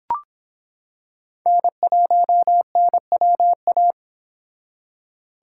Morse code, heard as a keyed tone, about 26 wpm, 700 Hz. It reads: N1NWA